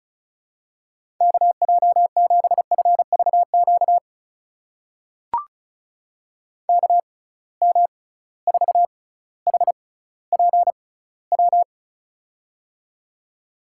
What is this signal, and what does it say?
Morse code, heard as a keyed tone, about 36 wpm, spaced as for 18 wpm, 700 Hz. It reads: KJ7FVQ KM4HPW
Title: USA Call Signs - 6 Characters - 6x Character Spacing - 35wpm